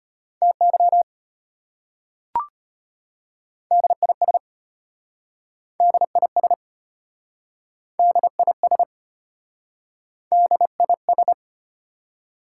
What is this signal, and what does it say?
Morse code, heard as a keyed tone, about 32 wpm, 690 Hz. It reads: TY DIS DIS DIS DIS